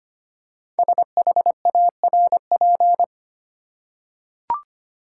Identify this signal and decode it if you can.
Morse code, heard as a keyed tone, about 25 wpm, 700 Hz. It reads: SHARP
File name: 5 Letter Words - First 3 Letters Given - 25wpm